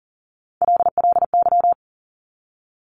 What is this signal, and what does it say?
Morse code, heard as a keyed tone, about 42 wpm, 700 Hz. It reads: LLY
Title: Top Three Letter Combinations in Words 40wpm